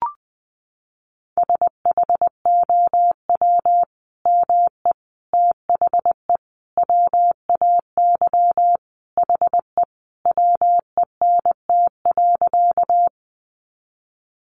Morse code, heard as a keyed tone, about 19 wpm, 700 Hz. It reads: SHOW ME THE WAY HE WENT.
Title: Sentences from Top 200 Words 20wpm